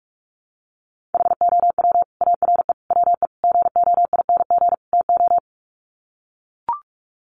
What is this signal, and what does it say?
Morse code, heard as a keyed tone, about 44 wpm, 700 Hz. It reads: HOW ARE WE GOING TO
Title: Sets of 5 Words Encore 45wpm